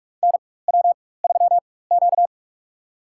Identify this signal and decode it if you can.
Morse code, heard as a keyed tone, about 46 wpm, spaced as for 21 wpm, 700 Hz. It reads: NW3Q
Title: USA Call Signs - 4 Characters - 4x Character Spacing - 45wpm